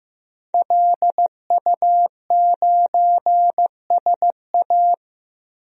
Morse code, heard as a keyed tone, about 15 wpm, 700 Hz. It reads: LU9SA
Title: DXCC-100 Call Signs - 5 Characters - 15wpm